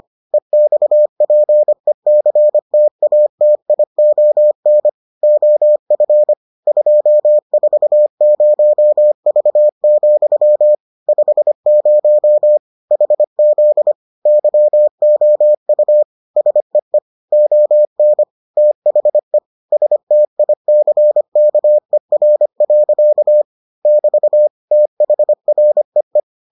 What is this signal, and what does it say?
Morse code, heard as a keyed tone, about 25 wpm, 600 Hz. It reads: EXPECTATION OF 240V, 50 HZ YOU SEE ON THE STICKER. <BT> THREE